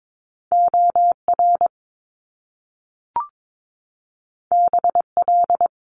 Morse code, heard as a keyed tone, about 22 wpm, 700 Hz. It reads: OR BL
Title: Top Two Letter Combinations in Words 22wpm